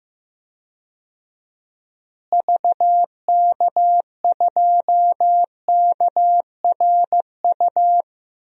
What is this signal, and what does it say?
Morse code, heard as a keyed tone, about 15 wpm, 700 Hz. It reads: VK2KRU
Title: DXCC-100 Call Signs - 6 Characters - Repeated 2x - 15wpm